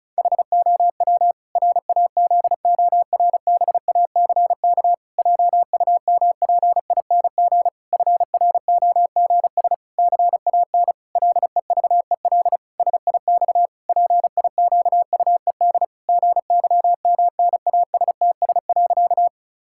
Morse code, heard as a keyed tone, about 36 wpm, 700 Hz. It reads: HOW RAZORBACK JUMPING FROGS CAN LEVEL SIX PIQUED GYMNASTS.
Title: Pangram Challenge v2 - 35wpm